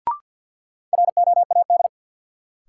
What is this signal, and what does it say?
Morse code, heard as a keyed tone, about 48 wpm, 700 Hz. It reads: ROAD
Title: Sets of 5 Words - New Word Drill - 50wpm